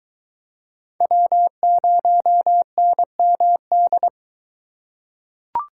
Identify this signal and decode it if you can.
Morse code, heard as a keyed tone, about 23 wpm, 700 Hz. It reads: W0NMD